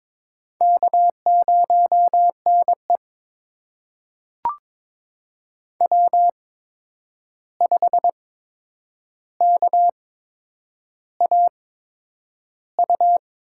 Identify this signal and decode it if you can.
Morse code, heard as a keyed tone, about 22 wpm, spaced as for 9 wpm, 700 Hz. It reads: K0NE W5KAU
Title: USA Call Signs - 8x Character Spacing - 22wpm